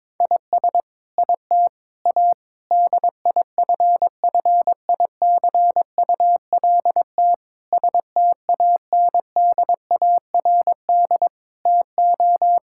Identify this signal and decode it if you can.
Morse code, heard as a keyed tone, about 22 wpm, 700 Hz. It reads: IS IT A DIFFICULT STANDARD TO
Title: Sentences from Top 1000 Words - Review Part 6 - 22wpm